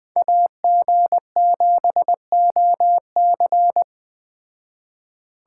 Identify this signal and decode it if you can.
Morse code, heard as a keyed tone, about 20 wpm, 700 Hz. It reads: AG7OC
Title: Speed Racing - USA 2x2 Call Signs 17wpm